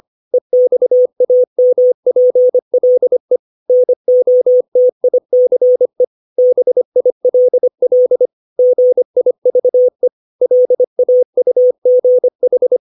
Morse code, heard as a keyed tone, about 25 wpm, 500 Hz. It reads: EXAMPLE NOTICE BILL GIVE LAUGH
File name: 632 words / 632 Words 25WPM